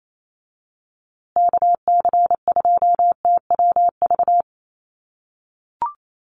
Morse code, heard as a keyed tone, about 28 wpm, 700 Hz. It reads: KC2TWV
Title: USA Call Signs Repeated 3x 28wpm